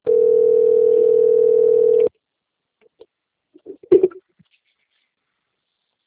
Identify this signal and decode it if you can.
Ringback tone (North America), 440 + 480 Hz.
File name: rg-1600-unknown- - - .11937